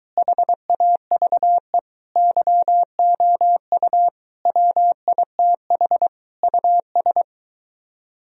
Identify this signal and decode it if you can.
Morse code, heard as a keyed tone, about 23 wpm, 700 Hz. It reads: HAVE YOU WITH US